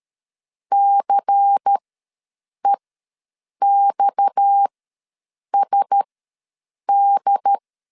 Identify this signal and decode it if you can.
Morse code, heard as a keyed tone, about 13 wpm, spaced as for 7 wpm, 800 Hz. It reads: CEXSD
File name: LEZIONI / 6 PKX0 / 6 PKX ws 12wpm 3106